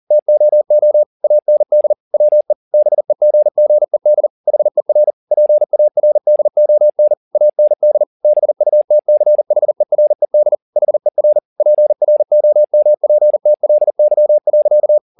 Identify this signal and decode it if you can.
Morse code, heard as a keyed tone, about 40 wpm, 600 Hz. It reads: TOO AND WE BEGGED HER PARDON AND BUTCHERED HER PROMPTLY.